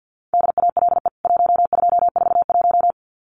Morse code, heard as a keyed tone, about 50 wpm, 700 Hz. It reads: DALE 1241
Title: CWT Exchange 50wpm